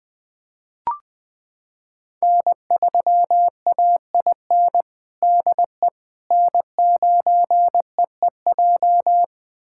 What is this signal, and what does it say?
Morse code, heard as a keyed tone, about 19 wpm, 700 Hz. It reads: N3AIN DE N9EEJ
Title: USA Callsign DE Callsign 20wpm